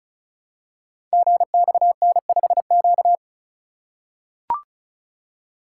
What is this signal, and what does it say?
Morse code, heard as a keyed tone, about 36 wpm, 700 Hz. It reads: GXN5Q
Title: Code Groups - 5 Letter:Numbers - 35wpm